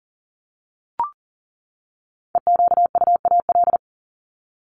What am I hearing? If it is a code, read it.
Morse code, heard as a keyed tone, about 40 wpm, 700 Hz. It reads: EQUAL
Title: LY Suffix - Top 100 Words with Base Words 40wpm